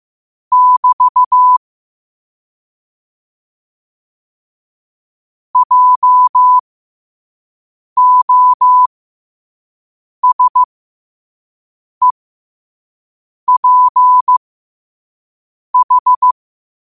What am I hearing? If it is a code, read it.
Morse code, heard as a keyed tone, about 15 wpm, spaced as for 5 wpm, 1000 Hz. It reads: <BT> JOSEPH
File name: qotdfiles / 5wpm